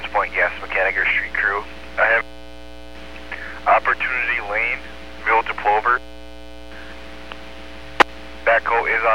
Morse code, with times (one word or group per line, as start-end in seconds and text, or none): none